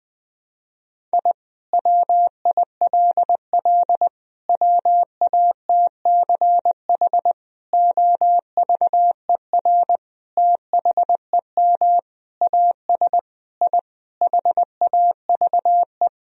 Morse code, heard as a keyed tone, about 20 wpm, 700 Hz. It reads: I WILL WATCH OVER THEM AS I HAVE
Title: Sentences from Top 1000 Words - Review Part 1 - 20wpm